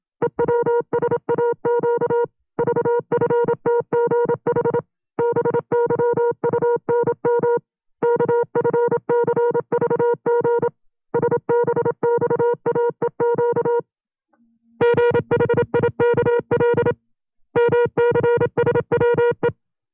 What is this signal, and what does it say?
Morse code, heard as a keyed tone, about 27 wpm, 480 Hz. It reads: EWSAQ VFTGH BYUNM KFCVG SBXAEQ GHIKL MCSWE
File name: CW-practice/25WPM